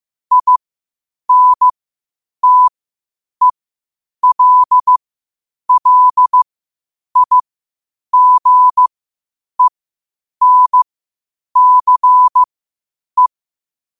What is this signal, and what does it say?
Morse code, heard as a keyed tone, about 15 wpm, spaced as for 8 wpm, 1000 Hz. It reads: INTELLIGENCE
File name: qotdfiles / 8wpm